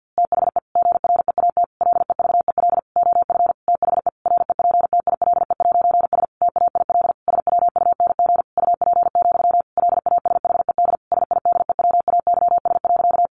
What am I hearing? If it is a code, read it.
Morse code, heard as a keyed tone, about 49 wpm, 700 Hz. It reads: THE GREAT LEVEL OF THE REPTILE'S TAIL SWUNG UP, LASHED SIDEWAYS.
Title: A Sound Of Thunder 50wpm